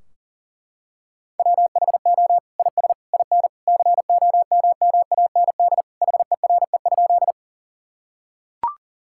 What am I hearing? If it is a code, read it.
Morse code, heard as a keyed tone, about 40 wpm, 700 Hz. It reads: WHO IS IN COMMAND HERE?
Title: Sentences from Top 1600 Words 40wpm